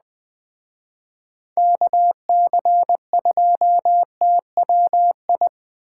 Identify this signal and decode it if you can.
Morse code, heard as a keyed tone, about 21 wpm, 700 Hz. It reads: KC2TWI